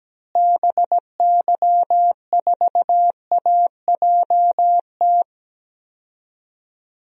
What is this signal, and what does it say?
Morse code, heard as a keyed tone, about 17 wpm, 700 Hz. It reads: BY4AJT
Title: DXCC-100 Call Signs - 6 Characters - Repeated 2x - 17wpm